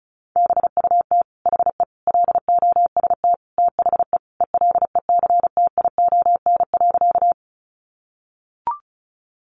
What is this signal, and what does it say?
Morse code, heard as a keyed tone, about 34 wpm, 700 Hz. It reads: BUT HE LOST THE ELECTION.
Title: Sentences from Top 1500 Words 35wpm